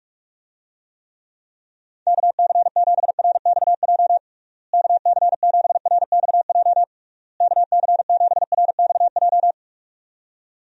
Morse code, heard as a keyed tone, about 44 wpm, 700 Hz. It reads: KC7RXJ KC7RXJ KC7RXJ
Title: USA Call Signs - 6 Characters - Repeated 3x - 45wpm